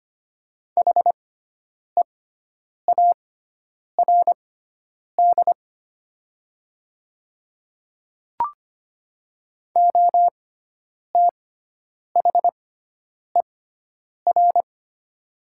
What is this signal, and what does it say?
Morse code, heard as a keyed tone, about 25 wpm, spaced as for 9 wpm, 700 Hz. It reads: HEARD OTHER
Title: 6x Character Spacing - Top 500 Words - 25wpm